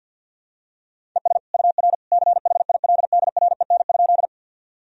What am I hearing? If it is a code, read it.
Morse code, heard as a keyed tone, about 52 wpm, 700 Hz. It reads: ES UR CHILDREN?
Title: Sentences from Top 300 Words and Abbreviations 50wpm